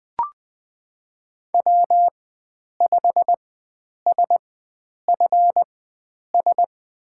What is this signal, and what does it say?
Morse code, heard as a keyed tone, about 20 wpm, spaced as for 9 wpm, 700 Hz. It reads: W5SFS
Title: USA Call Signs - 5 Characters - 4x Character Spacing - 20wpm